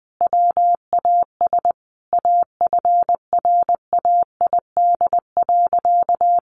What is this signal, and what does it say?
Morse code, heard as a keyed tone, about 20 wpm, 700 Hz. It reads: WAS AFRAID.
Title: Sentences from Top 1500 Words 20wpm